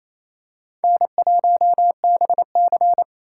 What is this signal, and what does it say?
Morse code, heard as a keyed tone, about 28 wpm, 700 Hz. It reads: N1BC
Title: USA Call Signs - 4 Characters - 6x Character Spacing - 28wpm